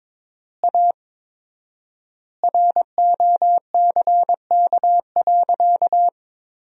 Morse code, heard as a keyed tone, about 23 wpm, 700 Hz. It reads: A ROCK.
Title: 4x Word-Spacing - Sentences from Top 400 Words - 22wpm